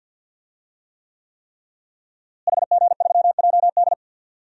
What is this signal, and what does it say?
Morse code, heard as a keyed tone, about 46 wpm, 700 Hz. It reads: HG3JD
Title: DXCC-100 Call Signs - 5 Characters - Repeated 3x - 50wpm